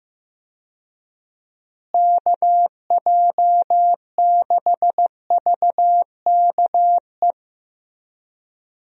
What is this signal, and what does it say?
Morse code, heard as a keyed tone, about 16 wpm, 700 Hz. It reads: KJ6VKE
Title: USA Call Signs - 6 Characters - Repeated 3x - 15wpm